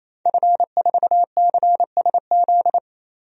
Morse code, heard as a keyed tone, about 27 wpm, 700 Hz. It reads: F4CSZ
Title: DXCC-100 Call Signs - 5 Characters - 28wpm